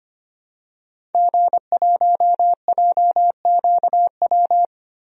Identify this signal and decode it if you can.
Morse code, heard as a keyed tone, about 25 wpm, 700 Hz. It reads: G1JQW
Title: DXCC-100 Call Signs - 5 Characters - Repeated 3x - 25wpm